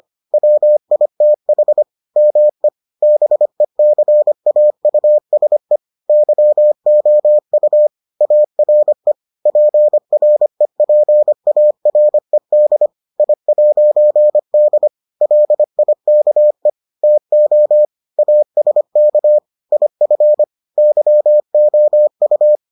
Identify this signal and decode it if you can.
Morse code, heard as a keyed tone, about 25 wpm, 600 Hz. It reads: WITH ME BECAUSE YOU ARE PREPARED I'D LIKE TO ASK IF YOU